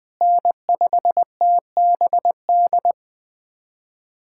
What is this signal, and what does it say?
Morse code, heard as a keyed tone, about 20 wpm, 700 Hz. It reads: N5TBD
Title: USA Callsign DE Callsign - 5 Characters - 20wpm